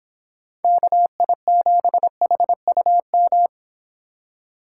Morse code, heard as a keyed tone, about 26 wpm, 700 Hz. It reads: KI7HUM